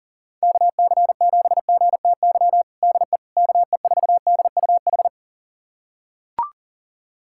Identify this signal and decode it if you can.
Morse code, heard as a keyed tone, about 40 wpm, 700 Hz. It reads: KC7GTY DE KE4DUH
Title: USA Callsign DE Callsign - 6 Characters - 40wpm